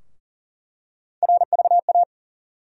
Morse code, heard as a keyed tone, about 40 wpm, 700 Hz. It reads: RVA
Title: USA Call Sign Postfixes - 3 Letters 40wpm